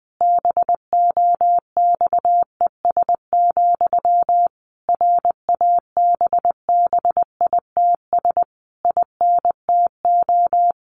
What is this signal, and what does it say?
Morse code, heard as a keyed tone, about 20 wpm, 700 Hz. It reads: BOXES, RABBITS INTO
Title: A Sound Of Thunder 20wpm